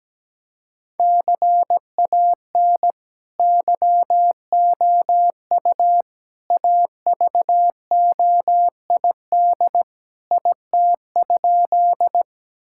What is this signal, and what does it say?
Morse code, heard as a keyed tone, about 17 wpm, 700 Hz. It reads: CAN YOU AVOID IT?
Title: Sentences from Top 1000 Words - Review Part 6 - 17wpm